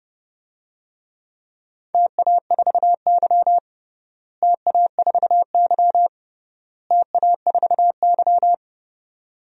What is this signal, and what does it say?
Morse code, heard as a keyed tone, about 31 wpm, 700 Hz. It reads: TA4Y TA4Y TA4Y
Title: DXCC-100 Call Signs - 4 Characters - Repeated 3x - 30wpm